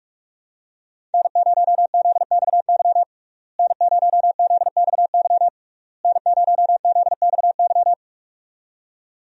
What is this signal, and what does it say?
Morse code, heard as a keyed tone, about 47 wpm, 700 Hz. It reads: N0ZXY N0ZXY N0ZXY
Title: USA Call Signs - 5 Characters - Repeated 3x - 45wpm